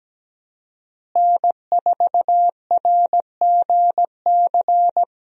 Morse code, heard as a keyed tone, about 17 wpm, 700 Hz. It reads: N4RGC
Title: USA Call Signs - 5 Characters - Repeated 3x - 17wpm